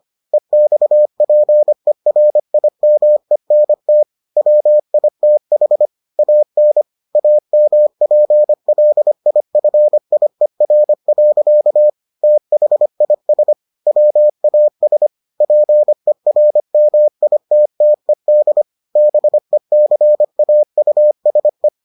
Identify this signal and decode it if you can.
Morse code, heard as a keyed tone, about 25 wpm, 600 Hz. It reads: EXPERIMENT WITH AN AMPLIFIER. THIS WAS PERMITTED BECAUSE